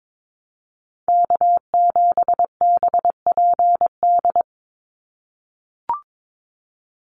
Morse code, heard as a keyed tone, about 22 wpm, 700 Hz. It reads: K7BPD
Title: USA Call Signs 22wpm